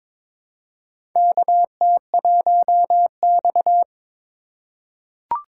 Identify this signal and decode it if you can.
Morse code, heard as a keyed tone, about 22 wpm, 700 Hz. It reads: KT1X
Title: USA Call Signs - 2x Character Spacing - 22wpm